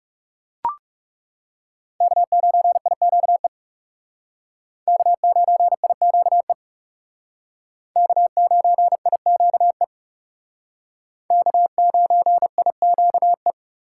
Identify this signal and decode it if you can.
Morse code, heard as a keyed tone, about 36 wpm, 700 Hz. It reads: K9IQE K9IQE K9IQE K9IQE
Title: Speed Racing - USA 1x3 Call Signs 30wpm